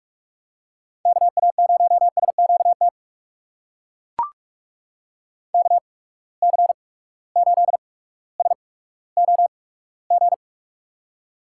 Morse code, heard as a keyed tone, about 42 wpm, spaced as for 19 wpm, 700 Hz. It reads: KA0SQT KC8SOG
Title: USA Call Signs - 6 Characters - 8x Character Spacing - 45wpm